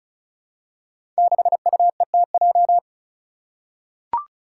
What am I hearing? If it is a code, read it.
Morse code, heard as a keyed tone, about 36 wpm, 700 Hz. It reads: 6UETJ
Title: Five Challenging Letter-Number Combinations 35wpm